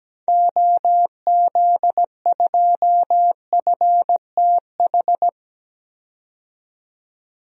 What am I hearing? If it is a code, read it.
Morse code, heard as a keyed tone, about 17 wpm, 700 Hz. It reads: OZ2FTH
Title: DXCC-100 Call Signs - 6 Characters - 17wpm